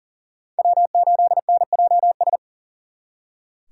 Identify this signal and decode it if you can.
Morse code, heard as a keyed tone, about 52 wpm, 700 Hz. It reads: W8NJS